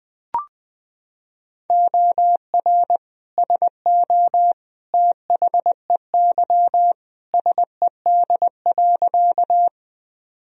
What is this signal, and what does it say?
Morse code, heard as a keyed tone, about 21 wpm, 700 Hz. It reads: OR SO THEY SED.
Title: Sentences from Top 100 Words and Abbreviations 20wpm